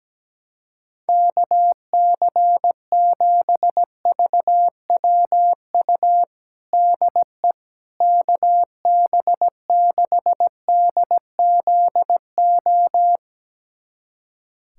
Morse code, heard as a keyed tone, about 17 wpm, 700 Hz. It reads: KC7VWU DE KB6DZO
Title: USA Callsign DE Callsign 17wpm